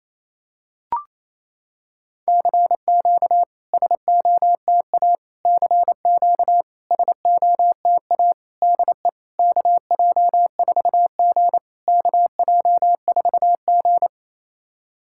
Morse code, heard as a keyed tone, about 28 wpm, 700 Hz. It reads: CQ SOTA CQ SOTA DE KJ4G KJ4G
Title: SOTA QSO Practice 28wpm